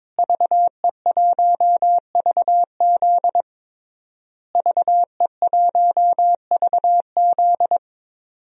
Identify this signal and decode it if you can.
Morse code, heard as a keyed tone, about 22 wpm, 700 Hz. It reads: VE1VZ VE1VZ
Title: DXCC-100 Call Signs - 5 Characters - Repeated 3x - 22wpm